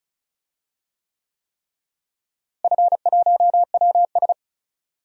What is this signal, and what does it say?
Morse code, heard as a keyed tone, about 34 wpm, 700 Hz. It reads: F1WS
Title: DXCC-100 Call Signs - 4 Characters - Repeated 3x - 35wpm